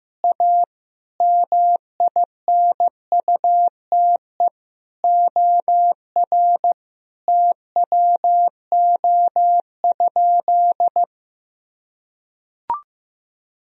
Morse code, heard as a keyed tone, about 15 wpm, 700 Hz. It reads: A MINUTE OR TWO?
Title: Sentences from Top 1000 Words - Review Part 1 - 15wpm